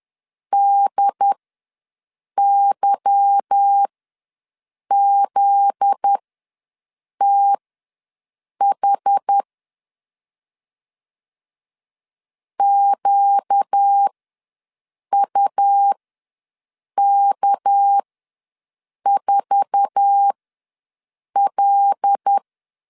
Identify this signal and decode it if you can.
Morse code, heard as a keyed tone, about 10 wpm, spaced as for 6 wpm, 800 Hz. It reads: DYZTH QUK4L
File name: LEZIONI / 8 YJ45 / 8 YJ4 ws 10wpm 4020